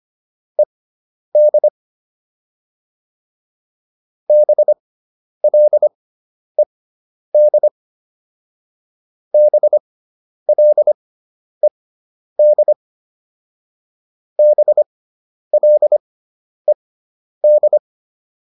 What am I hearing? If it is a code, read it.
Morse code, heard as a keyed tone, about 25 wpm, spaced as for 10 wpm, 600 Hz. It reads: ED BLED BLED BLED